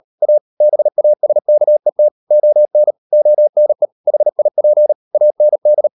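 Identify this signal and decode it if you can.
Morse code, heard as a keyed tone, about 38 wpm, 600 Hz. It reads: A BASKET ON ONE HIP AND
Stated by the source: Morse code practice at 35–39 wpm sending English text of Ken Follett's novel Never, from CW med Gnister.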